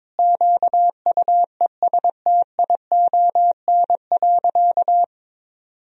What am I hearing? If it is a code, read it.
Morse code, heard as a keyed tone, about 22 wpm, 700 Hz. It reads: QUESTION.
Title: Sentences from Top 1000 Words - Review Part 3 - 22wpm